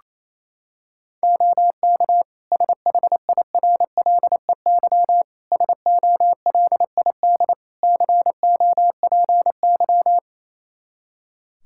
Morse code, heard as a keyed tone, about 28 wpm, 700 Hz. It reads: OK SHIRLEY SOLID COPY
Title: Practice QSO - Exchange 3 - More Info - 28wpm